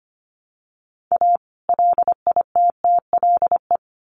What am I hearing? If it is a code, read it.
Morse code, heard as a keyed tone, about 25 wpm, 700 Hz. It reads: A LITTLE
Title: Sets of 2 Words 25wpm